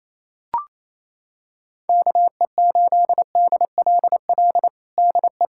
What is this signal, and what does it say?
Morse code, heard as a keyed tone, about 28 wpm, 700 Hz. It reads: KE8DLL DE
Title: USA Callsign DE Callsign 28wpm